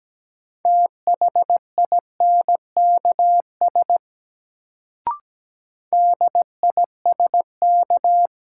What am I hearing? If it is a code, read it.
Morse code, heard as a keyed tone, about 17 wpm, 700 Hz. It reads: THINKS DISK
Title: Sentences from Top 1000 Words - Review Part 6 - New Word Drill - 17wpm